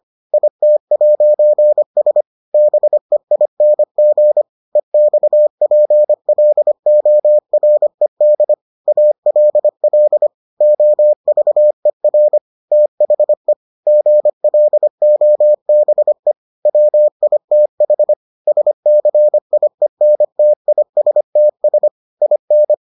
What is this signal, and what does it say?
Morse code, heard as a keyed tone, about 25 wpm, 600 Hz. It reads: IT'S BEING EXPLORED ALL OVER THE GLOBE WITH SCIENTISTS IN